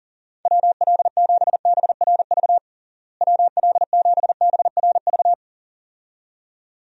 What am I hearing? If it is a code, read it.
Morse code, heard as a keyed tone, about 37 wpm, 700 Hz. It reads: WL7BRV WL7BRV
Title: USA Call Signs - 6 Characters - Repeated 2x - 40wpm